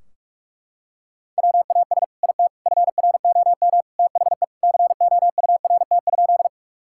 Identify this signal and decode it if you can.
Morse code, heard as a keyed tone, about 41 wpm, 700 Hz. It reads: WAS IT FROM THE COURT?
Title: Sentences from Top 1600 Words 45wpm